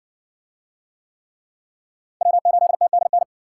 Morse code, heard as a keyed tone, about 60 wpm, 700 Hz. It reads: W8TDN